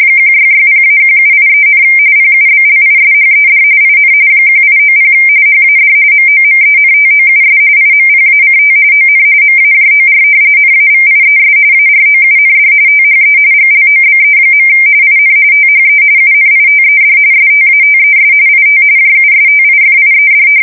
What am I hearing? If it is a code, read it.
RTTY, about 45.45 baud. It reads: 1660-1783, PUBLISHED IN 1890 HAD MOMENTOUS IMPACT ON MAJOR NAVIES AROUND THE GLOBE. IN THE UNITED STATES IT JUSTIFIED
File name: fsk170 / usnavy3